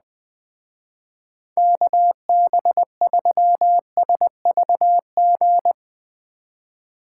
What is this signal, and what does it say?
Morse code, heard as a keyed tone, about 20 wpm, 700 Hz. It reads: KB3SVG
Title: Speed Racing - USA 2x3 Call Signs 17wpm